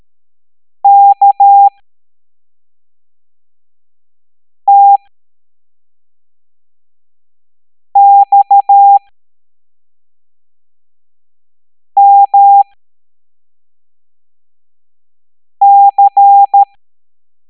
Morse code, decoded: KTXMC